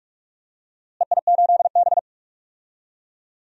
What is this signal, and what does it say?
Morse code, heard as a keyed tone, about 42 wpm, 700 Hz. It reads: EI8B
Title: DXCC-100 Call Signs - 4 Characters - Repeated 3x - 45wpm